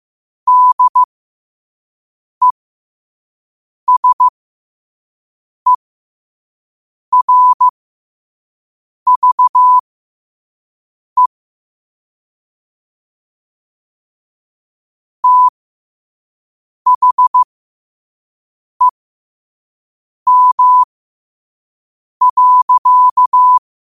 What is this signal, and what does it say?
Morse code, heard as a keyed tone, about 15 wpm, spaced as for 5 wpm, 1000 Hz. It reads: DESERVE THEM.
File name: qotdfiles / 5wpm